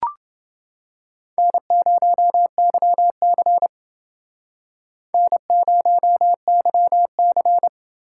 Morse code, heard as a keyed tone, about 29 wpm, 700 Hz. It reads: N0YC N0YC